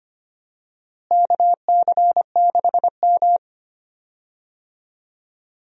Morse code, heard as a keyed tone, about 25 wpm, 700 Hz. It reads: KC6M
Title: USA Call Signs - 4 Characters - Repeated 3x - 25wpm